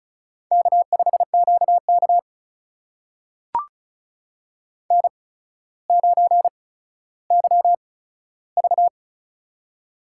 Morse code, heard as a keyed tone, about 34 wpm, spaced as for 15 wpm, 700 Hz. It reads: K5QK N9YV
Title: USA Call Signs - 8x Character Spacing - 35wpm